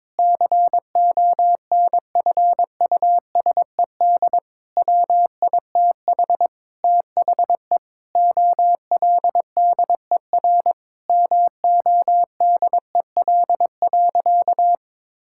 Morse code, heard as a keyed tone, about 22 wpm, 700 Hz. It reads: CONFUSED WITH THE OLDER MODEL.
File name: Sentences with 7-Word Phrases - 22wpm